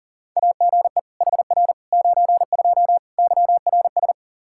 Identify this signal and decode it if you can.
Morse code, heard as a keyed tone, about 40 wpm, 700 Hz. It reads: AGE HR 92 YRS
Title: K1USN SST QSO Practice 40wpm